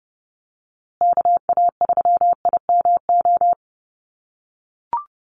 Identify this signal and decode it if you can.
Morse code, heard as a keyed tone, about 30 wpm, 700 Hz. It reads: KA3IMO
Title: USA Call Signs Repeated 3x 30wpm